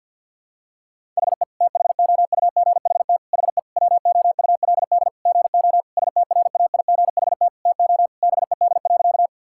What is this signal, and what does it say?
Morse code, heard as a keyed tone, about 50 wpm, 700 Hz. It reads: HE THOUGHT HE WOULD GO STRAIGHT TO BED.
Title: Sentences from Top 1000 Words - Review Part 6 - 50wpm